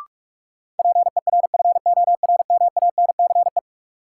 Morse code, heard as a keyed tone, about 46 wpm, 700 Hz. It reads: PERFORMANCE
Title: Sentences from Top 1000 Words - New Word Drill - 45wpm